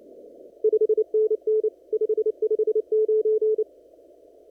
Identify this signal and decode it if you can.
Morse code, heard as a keyed tone, about 29 wpm, 430 Hz. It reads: 5NN 559
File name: CW / kt5j wpx cw-op-k5pi